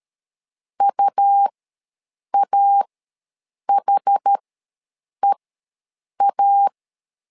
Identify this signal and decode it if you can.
Morse code, heard as a keyed tone, about 13 wpm, spaced as for 7 wpm, 800 Hz. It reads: UAHEA
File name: LEZIONI / 2 UHDN / 2 UHDN ws 12wpm 1708